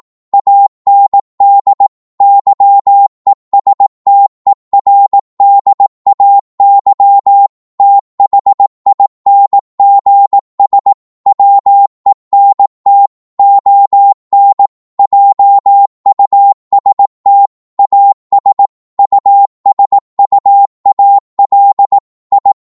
Morse code, heard as a keyed tone, about 18 wpm, 800 Hz. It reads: AND YESTERDAY THINGS WENT ON JUST AS USUAL I
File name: Morse / psg-18wpm